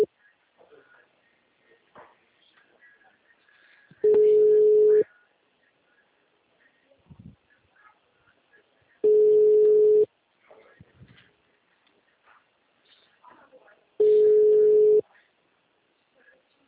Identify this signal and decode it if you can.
Ringback tone (Europe, 425 Hz), with speech.